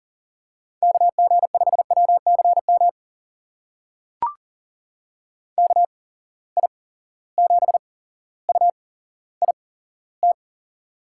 Morse code, heard as a keyed tone, about 40 wpm, spaced as for 17 wpm, 700 Hz. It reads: KG5WCM KI7UIT
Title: USA Call Signs - 6 Characters - 8x Character Spacing - 40wpm